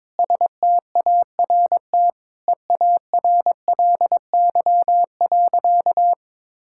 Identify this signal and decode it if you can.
Morse code, heard as a keyed tone, about 22 wpm, 700 Hz. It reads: START EARLY.
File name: Sentences with 9-Word Phrases - 22wpm